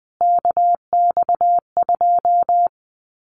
Morse code, heard as a keyed tone, about 20 wpm, 700 Hz. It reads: KX2
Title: Common QSO Phrases 20wpm